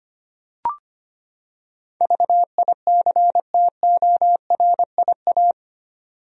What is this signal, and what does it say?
Morse code, heard as a keyed tone, about 25 wpm, 700 Hz. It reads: VICTORIA
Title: USA Common Names - 25wpm